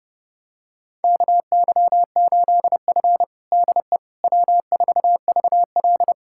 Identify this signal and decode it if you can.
Morse code, heard as a keyed tone, about 30 wpm, 700 Hz. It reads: KY8F DE W4VL
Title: USA Callsign DE Callsign - 4 Characters - 30wpm